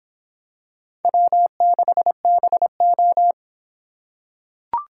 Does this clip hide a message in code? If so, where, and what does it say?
Yes: Morse code from 1.1 to 3.3 s, heard as a keyed tone, about 23 wpm, 700 Hz. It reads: W6BO